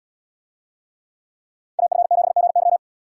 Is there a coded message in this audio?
Yes: Morse code from 1.8 to 2.8 s, heard as a keyed tone, about 77 wpm, 700 Hz. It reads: AF7GY